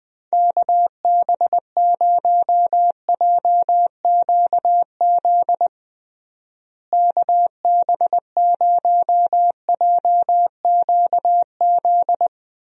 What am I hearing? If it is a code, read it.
Morse code, heard as a keyed tone, about 20 wpm, 700 Hz. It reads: KB0JQZ KB0JQZ
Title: USA Call Signs - 6 Characters - Repeated 3x - 20wpm